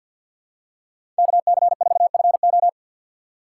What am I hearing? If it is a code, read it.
Morse code, heard as a keyed tone, about 47 wpm, 700 Hz. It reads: KC4FO
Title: USA Call Signs - 5 Characters - 4x Character Spacing - 50wpm